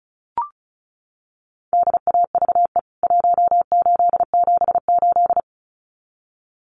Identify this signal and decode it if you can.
Morse code, heard as a keyed tone, about 34 wpm, 700 Hz. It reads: DAVE 1878
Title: CWT Exchange 35wpm